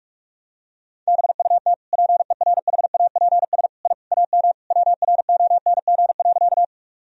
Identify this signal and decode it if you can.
Morse code, heard as a keyed tone, about 46 wpm, 700 Hz. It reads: BUT PERHAPS I AM WRONG.
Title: Sentences from Top 1000 Words - Review Part 2 - 45wpm